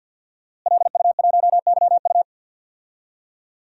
Morse code, heard as a keyed tone, about 52 wpm, 700 Hz. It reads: LU1YU
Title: DXCC-100 Call Signs - 5 Characters - Repeated 2x - 50wpm